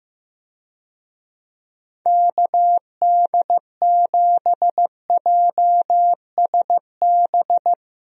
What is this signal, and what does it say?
Morse code, heard as a keyed tone, about 15 wpm, 700 Hz. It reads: KD7JSB
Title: USA Call Signs - 6 Characters - Repeated 3x - 15wpm